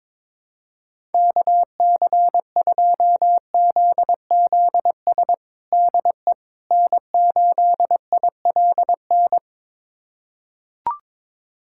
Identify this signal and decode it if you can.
Morse code, heard as a keyed tone, about 22 wpm, 700 Hz. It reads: KC2ZZS DE N8ILN
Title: USA Callsign DE Callsign 22wpm